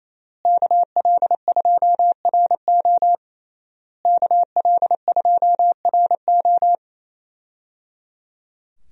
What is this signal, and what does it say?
Morse code, heard as a keyed tone, about 28 wpm, 700 Hz. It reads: KL2RO KL2RO
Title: USA Call Signs - 5 Characters - Repeated 3x - 28wpm